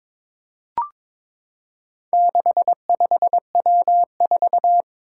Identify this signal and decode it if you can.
Morse code, heard as a keyed tone, about 22 wpm, 700 Hz. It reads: 65W4
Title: Four Challenging Letter-Number Combinations 22wpm